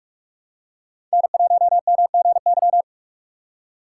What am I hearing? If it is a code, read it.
Morse code, heard as a keyed tone, about 46 wpm, 700 Hz. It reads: N1MGY
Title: USA Call Signs - 5 Characters - 2x Character Spacing - 45wpm